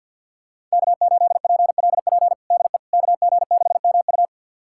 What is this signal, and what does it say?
Morse code, heard as a keyed tone, about 52 wpm, 700 Hz. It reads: K8PLP DE KG6MU
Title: USA Callsign DE Callsign - 5 Characters - 50wpm